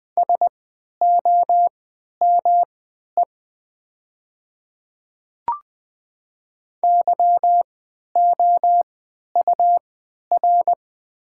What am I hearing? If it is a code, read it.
Morse code, heard as a keyed tone, about 20 wpm, spaced as for 11 wpm, 700 Hz. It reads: SOME YOUR
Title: 3x Character Spacing - Top 300 Words - 20wpm